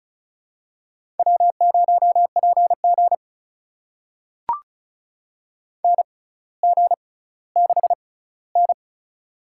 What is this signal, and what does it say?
Morse code, heard as a keyed tone, about 36 wpm, spaced as for 18 wpm, 700 Hz. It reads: W0PG NG6N
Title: USA Call Signs - 4 Characters - 6x Character Spacing - 35wpm